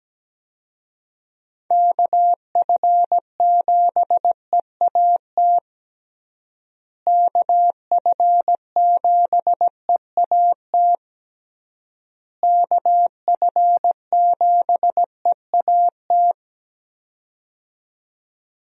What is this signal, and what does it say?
Morse code, heard as a keyed tone, about 17 wpm, 700 Hz. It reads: KF7EAT KF7EAT KF7EAT
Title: USA Call Signs - 6 Characters - Repeated 3x - 17wpm